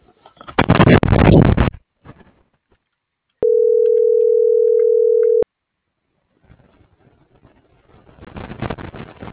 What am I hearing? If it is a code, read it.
Ringback tone (North America), 440 + 480 Hz.